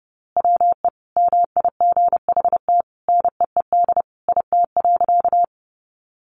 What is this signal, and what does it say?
Morse code, heard as a keyed tone, about 30 wpm, 700 Hz. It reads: WE MIGHT NEED IT.
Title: A Sound Of Thunder 30wpm